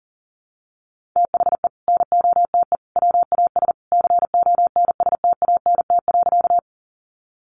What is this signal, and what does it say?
Morse code, heard as a keyed tone, about 40 wpm, 700 Hz. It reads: THE NOTE WAS CONSTANT.
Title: Sentences from Top 1500 Words 40wpm